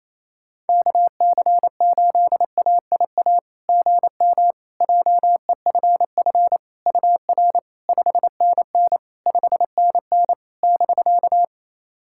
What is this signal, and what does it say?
Morse code, heard as a keyed tone, about 28 wpm, 700 Hz. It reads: KC8AIA GM JEFF UR 5NN 5NN <BK>
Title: SOTA QSO Practice 28wpm